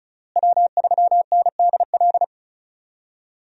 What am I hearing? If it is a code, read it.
Morse code, heard as a keyed tone, about 36 wpm, 700 Hz. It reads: W3NDL
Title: USA Call Signs - 5 Characters - Repeated 3x - 35wpm